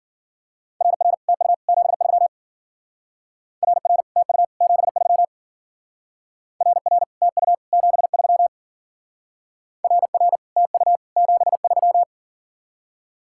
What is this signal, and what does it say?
Morse code, heard as a keyed tone, about 53 wpm, 700 Hz. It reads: RR TU 73 RR TU 73 RR TU 73 RR TU 73